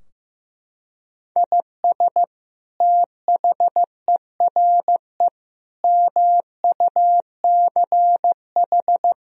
Morse code, heard as a keyed tone, about 15 wpm, 700 Hz. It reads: IS THERE MUCH
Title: Sentences from Top 1000 Words - Review Part 4 - 15wpm